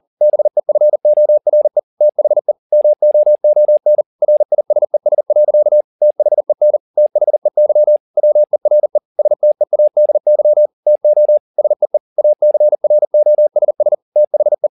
Morse code, heard as a keyed tone, about 40 wpm, 600 Hz. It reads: BEFORE THE MOON RISES. THEN THEY WERE STEADY TO SEE ACROSS THE
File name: the old man and the sea 2 40wpm